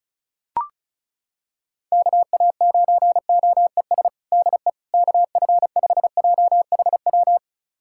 Morse code, heard as a keyed tone, about 34 wpm, 690 Hz. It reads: KA9OES DE KF5JHW
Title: USA Callsign DE Callsign 35wpm